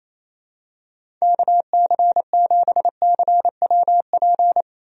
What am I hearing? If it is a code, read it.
Morse code, heard as a keyed tone, about 28 wpm, 700 Hz. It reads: KC7CWP
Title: USA Call Signs - 6 Characters - Repeated 3x - 28wpm